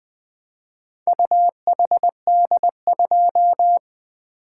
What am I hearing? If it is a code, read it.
Morse code, heard as a keyed tone, about 24 wpm, 700 Hz. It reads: UHD2
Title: Speed-Racing - Four Letter-Number Combinations - 20wpm